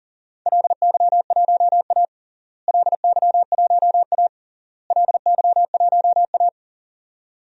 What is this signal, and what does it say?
Morse code, heard as a keyed tone, about 40 wpm, 700 Hz. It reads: LY1A LY1A LY1A
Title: DXCC-100 Call Signs - 4 Characters - Repeated 3x - 40wpm